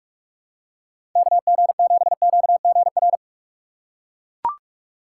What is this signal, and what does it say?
Morse code, heard as a keyed tone, about 42 wpm, 700 Hz. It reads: KG7QGR
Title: USA Call Signs - 4x Character Spacing - 45wpm